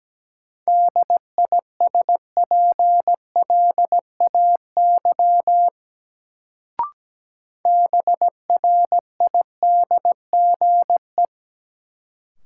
Morse code, heard as a keyed tone, about 17 wpm, 700 Hz. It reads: DISPLAY BRIDGE
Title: Sentences from Top 1200 Words - New Word Drill Alt - 17wpm